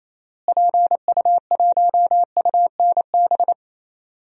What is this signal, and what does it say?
Morse code, heard as a keyed tone, about 28 wpm, 700 Hz. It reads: PU1UNB
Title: DXCC-100 Call Signs - 6 Characters - Repeated 2x - 28wpm